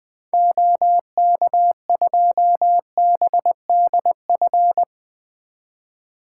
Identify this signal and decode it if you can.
Morse code, heard as a keyed tone, about 20 wpm, 700 Hz. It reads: OK2BDF
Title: 2023 SOTA Activator Call Signs - 20wpm